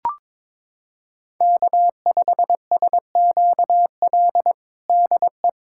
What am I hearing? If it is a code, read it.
Morse code, heard as a keyed tone, about 22 wpm, 700 Hz. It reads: K5SQL DE
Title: USA Callsign DE Callsign 22wpm